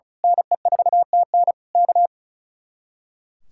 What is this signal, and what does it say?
Morse code, heard as a keyed tone, about 33 wpm, 700 Hz. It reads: NE4TN K
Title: Speed-Racing Common QSO Phrases - 35wpm